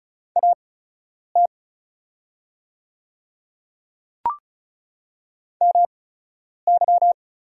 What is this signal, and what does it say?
Morse code, heard as a keyed tone, about 36 wpm, spaced as for 10 wpm, 700 Hz. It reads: AT MY
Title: 8x Character Spacing - Top 100 Words - 35wpm